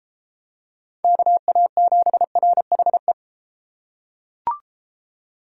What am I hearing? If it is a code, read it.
Morse code, heard as a keyed tone, about 34 wpm, 700 Hz. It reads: KA7RHE